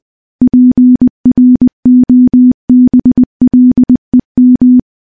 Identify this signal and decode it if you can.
Morse code, heard as a keyed tone, about 21 wpm, 250 Hz. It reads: PROBLEM
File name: morse-training-20WPM